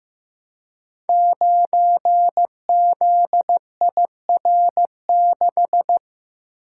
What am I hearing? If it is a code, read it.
Morse code, heard as a keyed tone, about 15 wpm, 700 Hz. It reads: 9ZIR6
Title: Code Groups - 5 Letter:Numbers - 15wpm